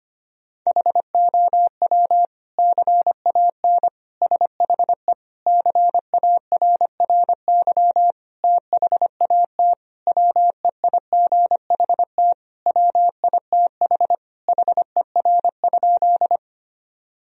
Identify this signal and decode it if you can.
Morse code, heard as a keyed tone, about 25 wpm, 700 Hz. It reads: HOW CAN SHE CARRY THAT WEIGHT WITH HER?
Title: Sentences from Top 1000 Words - Review Part 3 - 25wpm